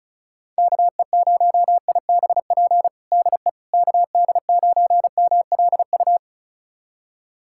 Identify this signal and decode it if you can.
Morse code, heard as a keyed tone, about 34 wpm, 700 Hz. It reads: KE0IBP DE KD9MLU
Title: USA Callsign DE Callsign - 6 Characters - 35wpm